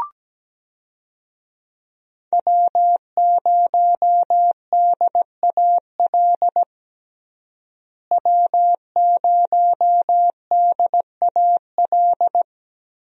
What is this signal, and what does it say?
Morse code, heard as a keyed tone, about 17 wpm, 700 Hz. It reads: W0DAL W0DAL
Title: USA Call Signs - 5 Characters - Repeated 3x - 17wpm